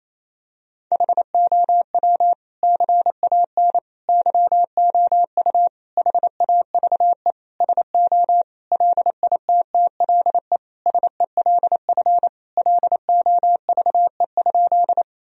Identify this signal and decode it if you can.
Morse code, heard as a keyed tone, about 27 wpm, 700 Hz. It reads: HOW CAN YOU HAVE SO LITTLE SELF LOVE?
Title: Sentences from Top 1000 Words - Review Part 1 - 28wpm